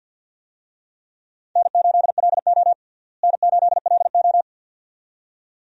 Morse code, heard as a keyed tone, about 52 wpm, 700 Hz. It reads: N8LO N8LO
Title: USA Call Signs - 4 Characters - Repeated 2x - 50wpm